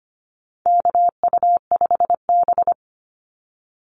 Morse code, heard as a keyed tone, about 25 wpm, 700 Hz. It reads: KU5B
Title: Call Signs - 25wpm